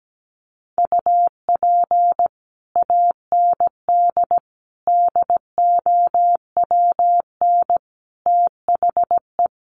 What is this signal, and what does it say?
Morse code, heard as a keyed tone, about 17 wpm, 700 Hz. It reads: UP AND DOWN THE
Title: Sets of 4 Words 17wpm